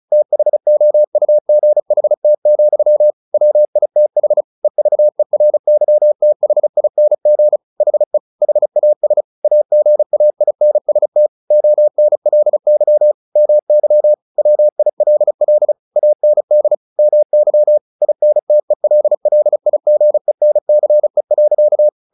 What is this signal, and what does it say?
Morse code, heard as a keyed tone, about 34 wpm, 600 Hz. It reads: THOUGHT, WITH EVERYTHING HE HAS AGAINST ONLY MY WILL AND MY INTELLIGENCE.
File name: the old man and the sea / the old man and the sea 1 35wpm